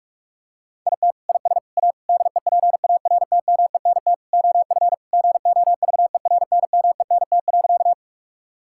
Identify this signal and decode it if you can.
Morse code, heard as a keyed tone, about 46 wpm, 700 Hz. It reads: IT IS A DEPARTMENT OF GOVERNMENT.
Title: Sentences from Top 1000 Words - Review Part 5 - 45wpm